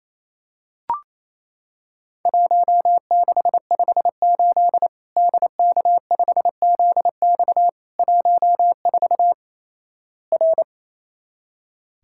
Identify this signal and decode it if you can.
Morse code, heard as a keyed tone, about 28 wpm, 700 Hz. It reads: 1658 DK5ZX 14 R
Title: Worked All Europe DX Contest - QTC Exchange - 28wpm